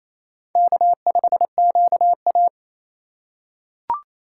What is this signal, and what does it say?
Morse code, heard as a keyed tone, about 28 wpm, 700 Hz. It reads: K5QA
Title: USA Call Signs - 4 Characters - 4x Character Spacing - 28wpm